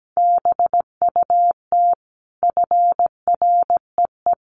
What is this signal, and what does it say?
Morse code, heard as a keyed tone, about 17 wpm, 700 Hz. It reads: BUT FREE
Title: Sentences from Top 1300 Words 17wpm